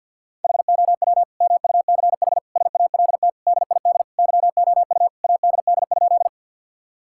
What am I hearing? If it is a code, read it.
Morse code, heard as a keyed tone, about 50 wpm, 700 Hz. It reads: HOW MUCH SALT DID YOU ADD?
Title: Sentences from Top 1000 Words - Review Part 5 - 50wpm